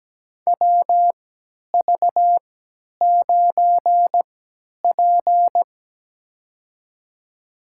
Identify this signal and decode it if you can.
Morse code, heard as a keyed tone, about 17 wpm, spaced as for 10 wpm, 700 Hz. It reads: WV9P
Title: USA Call Signs - 4 Characters - 3x Character Spacing - 17wpm